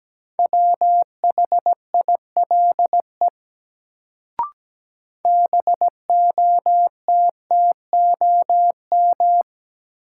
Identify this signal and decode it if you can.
Morse code, heard as a keyed tone, about 17 wpm, 700 Hz. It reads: WHILE BOTTOM
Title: Sets of 3 Words - New Word Drill - 17wpm